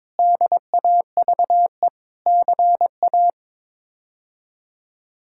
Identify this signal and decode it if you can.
Morse code, heard as a keyed tone, about 21 wpm, 700 Hz. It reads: DAVE CA
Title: K1USN SST QSO Practice - 22wpm